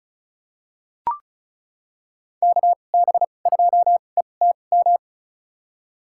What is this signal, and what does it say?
Morse code, heard as a keyed tone, about 34 wpm, spaced as for 25 wpm, 700 Hz. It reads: KB2ETM
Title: USA Call Signs - 6 Characters - 2x Character Spacing - 35wpm